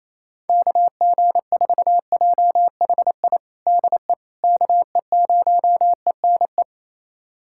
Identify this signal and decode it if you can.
Morse code, heard as a keyed tone, about 28 wpm, 700 Hz. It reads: KG4JHI DE KE0ENE
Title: USA Callsign DE Callsign 28wpm